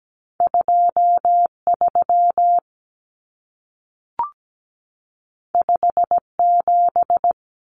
Morse code, heard as a keyed tone, about 17 wpm, 700 Hz. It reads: 2 3 5 7
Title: Common Cut Numbers - 2-Digit 17wpm